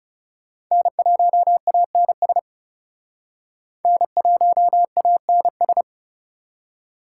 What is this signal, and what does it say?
Morse code, heard as a keyed tone, about 35 wpm, 700 Hz. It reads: N1ANS N1ANS
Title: Speed Racing - USA 1x3 Call Signs 30wpm